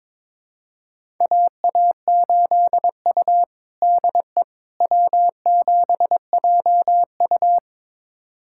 Morse code, heard as a keyed tone, about 22 wpm, 700 Hz. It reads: AA8U DE W7JU
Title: USA Callsign DE Callsign 22wpm